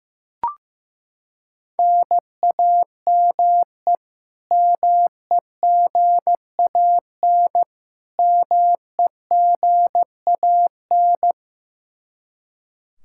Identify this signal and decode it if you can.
Morse code, heard as a keyed tone, about 15 wpm, 700 Hz. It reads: NAME MEGAN MEGAN
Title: Practice QSO - Exchange 1 - Beginning - 15wpm